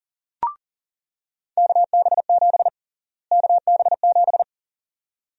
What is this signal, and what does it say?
Morse code, heard as a keyed tone, about 37 wpm, 700 Hz. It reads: KB7 KB7
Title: USA Call Sign Prefixes - Repeated 2x - 40wpm